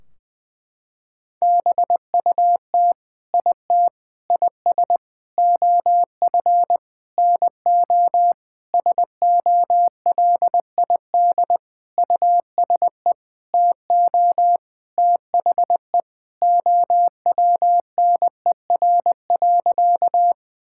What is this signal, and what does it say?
Morse code, heard as a keyed tone, about 20 wpm, 700 Hz. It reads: BUT IT IS OF NO SOLID USE TO THE OWNER.
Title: Sentences from Top 1600 Words 20wpm